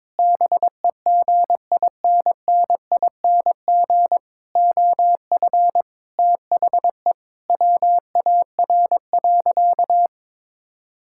Morse code, heard as a keyed tone, about 21 wpm, 700 Hz. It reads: BEGINNING OF THE WAR.
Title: Sentences from Top 1000 Words - Review Part 6 - 22wpm